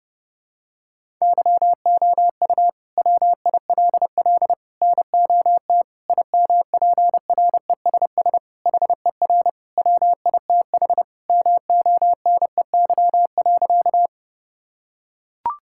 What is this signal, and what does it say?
Morse code, heard as a keyed tone, about 30 wpm, 700 Hz. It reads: YOU WILL NOT IMPRESS HER WITH MONEY.
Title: Sentences from Top 1600 Words 30wpm